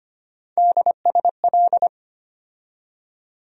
Morse code, heard as a keyed tone, about 25 wpm, 700 Hz. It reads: DSL
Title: USA Call Sign Postfixes - 3 Letters 25wpm